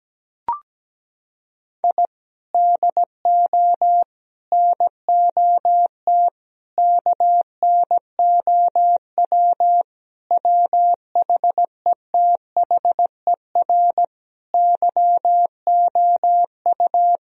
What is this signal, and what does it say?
Morse code, heard as a keyed tone, about 17 wpm, 700 Hz. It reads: I DO NOT KNOW WHETHER YOU
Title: Sentences from Top 1000 Words - Review Part 3 - 17wpm